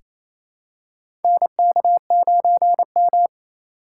Morse code, heard as a keyed tone, about 27 wpm, 700 Hz. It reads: NK9M
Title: USA Call Signs - 4 Characters - 3x Character Spacing - 28wpm